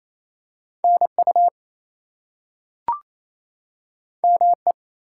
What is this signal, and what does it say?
Morse code, heard as a keyed tone, about 28 wpm, 700 Hz. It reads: NU ME
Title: US States and Canadian Provinces 28wpm